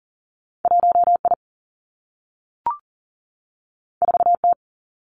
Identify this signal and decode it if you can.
Morse code, heard as a keyed tone, about 39 wpm, 700 Hz. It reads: 1 I 4 T
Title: Two Challenging Letter-Number Combinations 40wpm